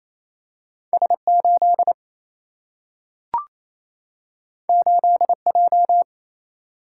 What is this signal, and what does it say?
Morse code, heard as a keyed tone, about 28 wpm, 700 Hz. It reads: S 8 8 J
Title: Single-Letter Single-Number 28wpm